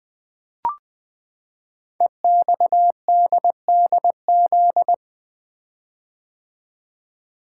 Morse code, heard as a keyed tone, about 20 wpm, 700 Hz. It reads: EXDDZ
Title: Code Groups - 5 Letters - 20wpm